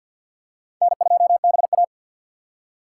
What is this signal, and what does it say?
Morse code, heard as a keyed tone, about 53 wpm, 700 Hz. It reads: N2BA
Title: USA Call Signs - 4 Characters - 3x Character Spacing - 50wpm